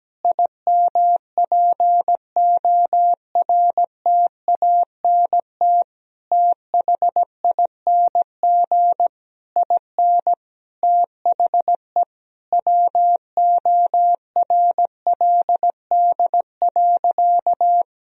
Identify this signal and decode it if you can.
Morse code, heard as a keyed tone, about 17 wpm, 700 Hz. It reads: IMPORTANT THING IN THE WORLD.
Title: Sentences from Top 1000 Words - Review Part 3 - 17wpm